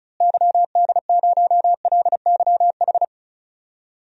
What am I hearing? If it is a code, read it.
Morse code, heard as a keyed tone, about 36 wpm, 700 Hz. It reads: YD0LYH
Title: DXCC-100 Call Signs - 6 Characters - Repeated 2x - 35wpm